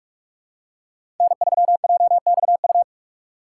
Morse code, heard as a keyed tone, about 46 wpm, 700 Hz. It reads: N3JXU
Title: USA Call Signs - 5 Characters - 3x Character Spacing - 45wpm